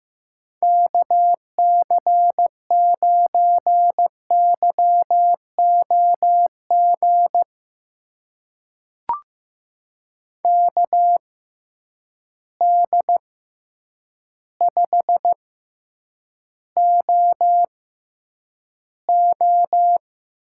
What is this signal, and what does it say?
Morse code, heard as a keyed tone, about 15 wpm, spaced as for 8 wpm, 700 Hz. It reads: KC9YOG KD5OO
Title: USA Call Signs - 6 Characters - 6x Character Spacing - 15wpm